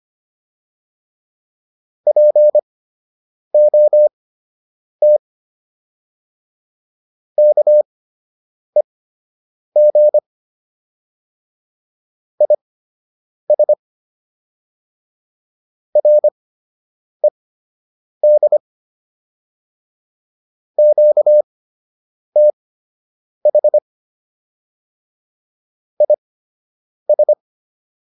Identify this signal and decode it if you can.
Morse code, heard as a keyed tone, about 25 wpm, spaced as for 8 wpm, 600 Hz. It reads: POT KEG IS RED QTH IS